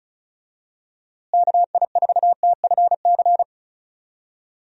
Morse code, heard as a keyed tone, about 34 wpm, 700 Hz. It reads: KI4TFC
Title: USA Call Signs - 6 Characters - Repeated 3x - 35wpm